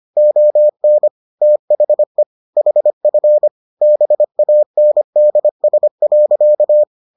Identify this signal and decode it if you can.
Morse code, heard as a keyed tone, about 25 wpm, 600 Hz. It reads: ON THE HF BANDS.